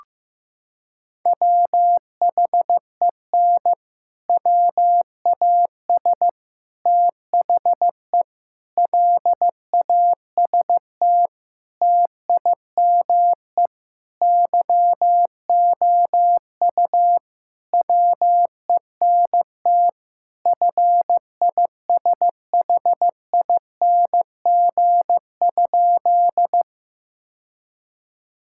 Morse code, heard as a keyed tone, about 15 wpm, 700 Hz. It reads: WHEN WAS THE LAST TIME YOU WENT FISHING?
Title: Sentences from Top 1000 Words - Review Part 5 - 15wpm